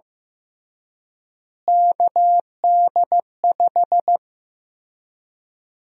Morse code, heard as a keyed tone, about 15 wpm, 700 Hz. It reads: KD5
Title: USA Call Sign Prefixes - Repeated 2x - 15wpm